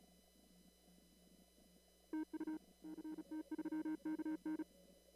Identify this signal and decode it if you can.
Morse code, heard as a keyed tone, about 34 wpm, 310 Hz. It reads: TU CT3KN
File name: CW / kt5j wpx cw-op-k5pi